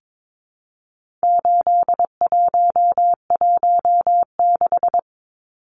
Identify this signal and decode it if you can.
Morse code, heard as a keyed tone, about 22 wpm, 700 Hz. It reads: 8116
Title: Numbers - 4-Digit 22wpm